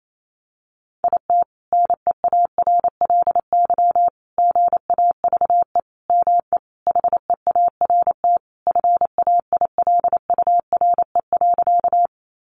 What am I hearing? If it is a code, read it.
Morse code, heard as a keyed tone, about 28 wpm, 700 Hz. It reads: IT NEARLY GAVE ME HEART FAILURE.
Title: Sentences from Top 1300 Words 28wpm